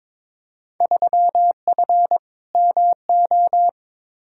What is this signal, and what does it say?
Morse code, heard as a keyed tone, about 21 wpm, 700 Hz. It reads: 3F MO
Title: Field Day Exchanges Encore - 22wpm